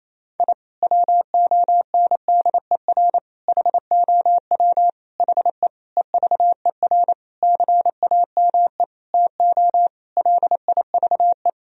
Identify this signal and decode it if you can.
Morse code, heard as a keyed tone, about 28 wpm, 700 Hz. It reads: I WONDER HOW HE EVER CAME TO LIVE
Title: Sentences from Top 1000 Words - Review Part 3 - 28wpm